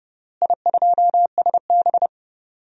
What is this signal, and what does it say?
Morse code, heard as a keyed tone, about 30 wpm, 700 Hz. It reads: I2SB
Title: DXCC-100 Call Signs - 4 Characters - Repeated 3x - 30wpm